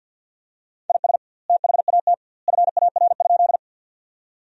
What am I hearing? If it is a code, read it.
Morse code, heard as a keyed tone, about 50 wpm, 700 Hz. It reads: IS THAT FAR?
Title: 2x Word-Spacing - Sentences from Top 300 Words - 50wpm